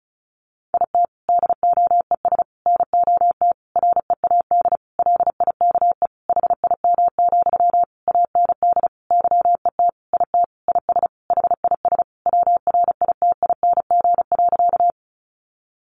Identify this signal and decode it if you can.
Morse code, heard as a keyed tone, about 34 wpm, 700 Hz. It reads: IT DOES NOT READ LIKE HIM, AND YET IT IS HIS WRITING.
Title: Sentences from Top 800 Words 35wpm